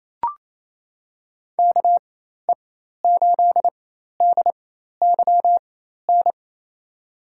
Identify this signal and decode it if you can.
Morse code, heard as a keyed tone, about 28 wpm, spaced as for 13 wpm, 700 Hz. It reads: KE8DYN
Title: USA Call Signs - 4x Character Spacing - 28wpm